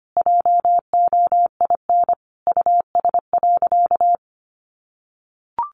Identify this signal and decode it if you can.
Morse code, heard as a keyed tone, about 25 wpm, 700 Hz. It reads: JOIN US.
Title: Sentences from Top 1400 Words 25wpm